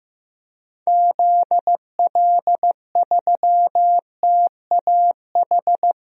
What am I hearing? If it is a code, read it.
Morse code, heard as a keyed tone, about 15 wpm, 700 Hz. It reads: ZL3TAH
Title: DXCC-100 Call Signs - 6 Characters - Repeated 3x - 15wpm